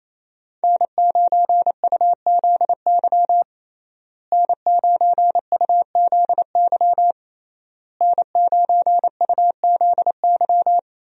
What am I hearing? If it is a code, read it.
Morse code, heard as a keyed tone, about 27 wpm, 700 Hz. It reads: N9UZY N9UZY N9UZY
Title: USA Call Signs - 5 Characters - Repeated 3x - 28wpm